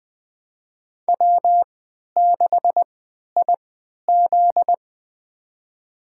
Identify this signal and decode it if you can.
Morse code, heard as a keyed tone, about 20 wpm, spaced as for 11 wpm, 700 Hz. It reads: W6IZ
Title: USA Call Signs - 4 Characters - 3x Character Spacing - 20wpm